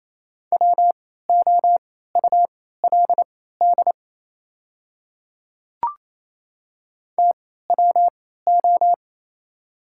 Morse code, heard as a keyed tone, about 28 wpm, spaced as for 16 wpm, 700 Hz. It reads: WOULD TWO
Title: 3x Character Spacing - Top 100 Words - 28wpm